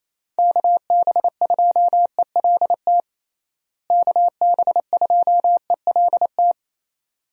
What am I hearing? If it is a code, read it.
Morse code, heard as a keyed tone, about 27 wpm, 700 Hz. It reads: KB2ELT KB2ELT
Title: USA Call Signs - 6 Characters - Repeated 2x - 28wpm